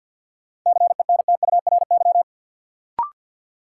Morse code, heard as a keyed tone, about 53 wpm, 700 Hz. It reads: CENTURY
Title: Sets of 5 Words - New Word Drill - 50wpm